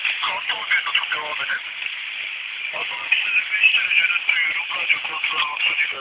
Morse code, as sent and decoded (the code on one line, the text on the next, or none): none